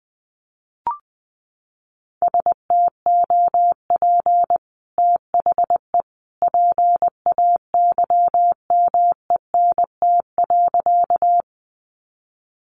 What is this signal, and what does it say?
Morse code, heard as a keyed tone, about 20 wpm, 700 Hz. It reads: STOP THE PAYMENT.
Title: Sentences from Top 1200 Words 20wpm